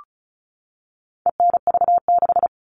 Morse code, heard as a keyed tone, about 34 wpm, 700 Hz. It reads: ENV6
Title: All Cut Numbers - 4-Digit 35wpm